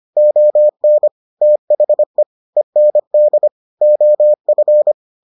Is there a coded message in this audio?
Yes: Morse code from 0.2 to 4.9 s, heard as a keyed tone, about 25 wpm, 600 Hz. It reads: ON THE END OF